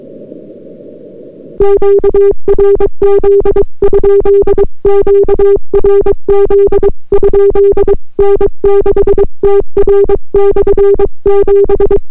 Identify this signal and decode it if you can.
Morse code, heard as a keyed tone, about 22 wpm, 400 Hz. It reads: QRZ? QRZ? N6TR/7